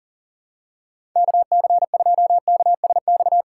Morse code, heard as a keyed tone, about 37 wpm, 700 Hz. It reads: KC2KSX
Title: USA Call Signs - 6 Characters - 3x Character Spacing - 40wpm